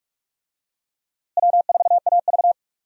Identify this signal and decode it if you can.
Morse code, heard as a keyed tone, about 46 wpm, 700 Hz. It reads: W4AV
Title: USA Call Signs - 4 Characters - Repeated 3x - 45wpm